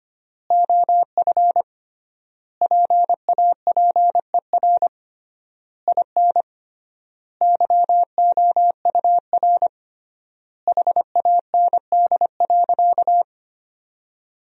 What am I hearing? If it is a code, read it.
Morse code, heard as a keyed tone, about 25 wpm, 700 Hz. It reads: OF PAPER IN YOUR HAND.
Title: 3x Word-Spacing - Sentences from Top 400 Words - 25wpm